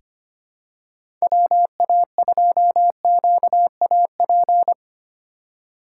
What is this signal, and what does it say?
Morse code, heard as a keyed tone, about 25 wpm, 700 Hz. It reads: WA2QAP
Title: USA Call Signs - 6 Characters - 2x Character Spacing - 25wpm